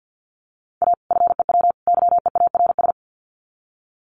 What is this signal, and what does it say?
Morse code, heard as a keyed tone, about 50 wpm, 700 Hz. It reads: A FEW YEARS
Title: Sets of 3 Words 50wpm